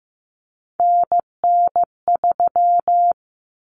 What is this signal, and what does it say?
Morse code, heard as a keyed tone, about 15 wpm, 700 Hz. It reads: NN3
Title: Common Cut Numbers - 3-Digit 15wpm